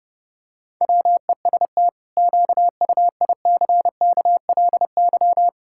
Morse code, heard as a keyed tone, about 32 wpm, 700 Hz. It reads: WEST QUICKLY
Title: Pangram Challenge v2 - 30wpm